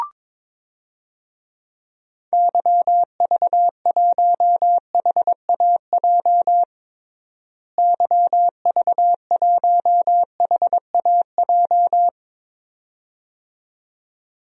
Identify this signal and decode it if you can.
Morse code, heard as a keyed tone, about 22 wpm, 700 Hz. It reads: YV1HAJ YV1HAJ
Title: DXCC-100 Call Signs - 6 Characters - Repeated 2x - 22wpm